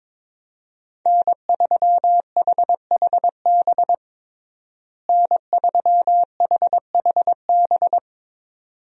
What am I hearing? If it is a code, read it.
Morse code, heard as a keyed tone, about 22 wpm, 700 Hz. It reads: N3HHB N3HHB
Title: USA Call Signs - 5 Characters - Repeated 2x - 22wpm